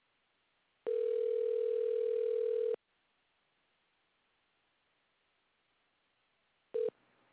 Ringback tone (North America), 440 + 480 Hz.